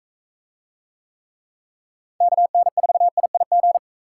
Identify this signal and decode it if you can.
Morse code, heard as a keyed tone, about 41 wpm, 700 Hz. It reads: KN4IIG